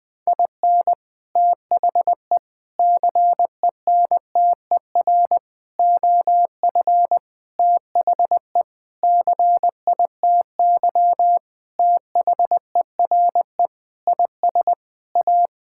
Morse code, heard as a keyed tone, about 21 wpm, 700 Hz. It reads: IN THE CENTER OF THE CITY THERE IS A
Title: Sentences with 5-Word Phrases - 20wpm